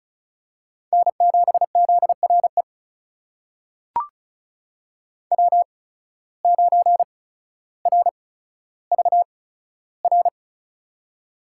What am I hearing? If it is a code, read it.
Morse code, heard as a keyed tone, about 36 wpm, spaced as for 15 wpm, 700 Hz. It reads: N7ZRE W9RVR
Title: USA Call Signs - 5 Characters - 8x Character Spacing - 35wpm